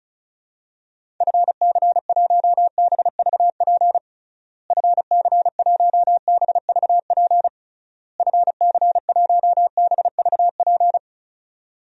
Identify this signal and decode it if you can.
Morse code, heard as a keyed tone, about 34 wpm, 700 Hz. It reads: FC1BVP FC1BVP FC1BVP
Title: DXCC-100 Call Signs - 6 Characters - Repeated 3x - 35wpm